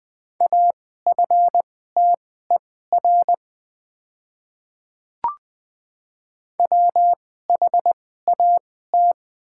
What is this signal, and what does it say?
Morse code, heard as a keyed tone, about 20 wpm, spaced as for 14 wpm, 700 Hz. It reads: AFTER WHAT
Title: 2x Character Spacing - Top 300 Words - 20wpm